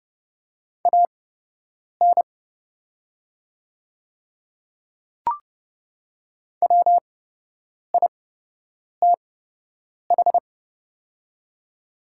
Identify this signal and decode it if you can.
Morse code, heard as a keyed tone, about 30 wpm, spaced as for 8 wpm, 700 Hz. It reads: AN WITH
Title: 8x Character Spacing - Top 100 Words - 30wpm